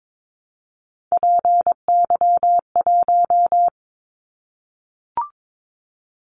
Morse code, heard as a keyed tone, about 22 wpm, 700 Hz. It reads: PY1
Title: Callsign Prefixes for Top 50 DXCC - 22wpm